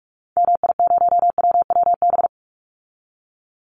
Morse code, heard as a keyed tone, about 44 wpm, 700 Hz. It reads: MI0WWB
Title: Call Signs - 45wpm